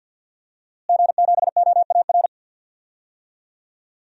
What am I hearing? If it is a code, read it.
Morse code, heard as a keyed tone, about 51 wpm, 700 Hz. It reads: G7OAR
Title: DXCC-100 Call Signs - 5 Characters - Repeated 3x - 50wpm